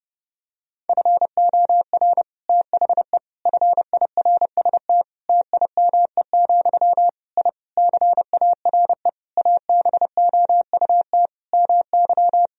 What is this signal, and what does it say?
Morse code, heard as a keyed tone, about 30 wpm, 700 Hz. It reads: FOR THE FIRST TIME, I CARE ABOUT MY
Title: Sentences from Top 1000 Words - Review Part 2 - 30wpm